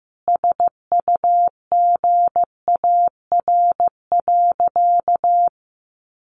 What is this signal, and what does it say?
Morse code, heard as a keyed tone, about 15 wpm, 700 Hz. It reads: SUGAR.
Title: Sentences from Top 1200 Words 15wpm